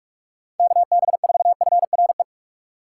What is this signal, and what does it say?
Morse code, heard as a keyed tone, about 47 wpm, 700 Hz. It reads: KB4FRE
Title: USA Call Signs - 6 Characters - 8x Character Spacing - 45wpm